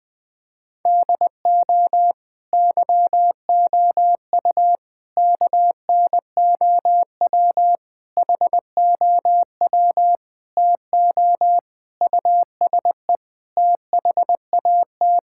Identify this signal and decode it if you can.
Morse code, heard as a keyed tone, about 20 wpm, 700 Hz. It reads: DO YOU KNOW HOW TO USE THAT
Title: Sentences from Top 1000 Words - Review Part 6 - 20wpm